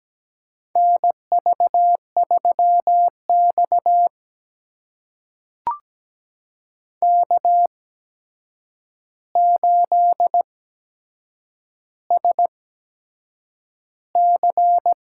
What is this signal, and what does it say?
Morse code, heard as a keyed tone, about 17 wpm, spaced as for 7 wpm, 700 Hz. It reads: NV3X K8SC
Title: USA Call Signs - 4 Characters - 8x Character Spacing - 17wpm